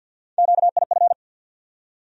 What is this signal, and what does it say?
Morse code, heard as a keyed tone, about 52 wpm, 700 Hz. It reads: QIF
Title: USA Call Sign Postfixes - 3 Letters 50wpm